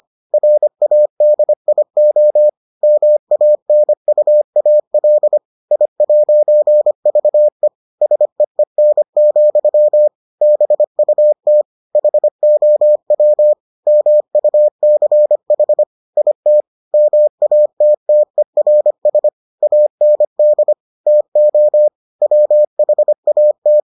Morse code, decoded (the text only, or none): RADIO MANUAL I'VE SEEN, BUT HOW MUCH IT MATTERS AND TO WHAT